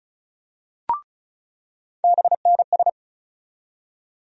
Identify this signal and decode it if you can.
Morse code, heard as a keyed tone, about 34 wpm, 700 Hz. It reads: BNS
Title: USA Call Sign Postfixes - 3 Letters 35wpm